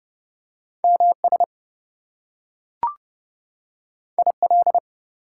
Morse code, heard as a keyed tone, about 30 wpm, 700 Hz. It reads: MS IL
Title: US States and Canadian Provinces 30wpm